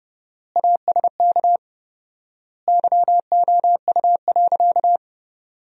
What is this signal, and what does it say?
Morse code, heard as a keyed tone, about 30 wpm, 700 Hz. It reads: ASK YOU.
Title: 4x Word-Spacing - Sentences from Top 400 Words - 30wpm